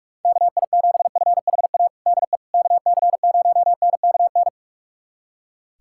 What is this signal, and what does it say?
Morse code, heard as a keyed tone, about 44 wpm, 690 Hz. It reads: KI7FHA DE KC0NKN
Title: USA Callsign DE Callsign 45wpm